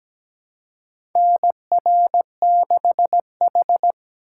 Morse code, heard as a keyed tone, about 17 wpm, 700 Hz. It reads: NR6H
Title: USA Call Signs - 4x Character Spacing - 17wpm